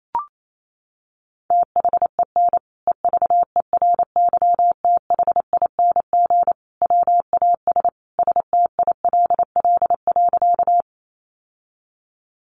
Morse code, heard as a keyed tone, about 28 wpm, 700 Hz. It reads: THEN EVERYTHING WAS STILL.
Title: Sentences from Top 700 Words 28wpm